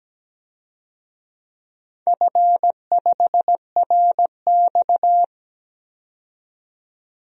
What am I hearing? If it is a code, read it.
Morse code, heard as a keyed tone, about 17 wpm, 700 Hz. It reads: F5RX
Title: DXCC-100 Call Signs - 4 Characters - 17wpm